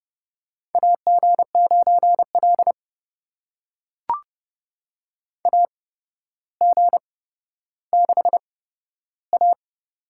Morse code, heard as a keyed tone, about 32 wpm, spaced as for 13 wpm, 700 Hz. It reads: AG9L AG6A
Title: USA Call Signs - 4 Characters - 8x Character Spacing - 30wpm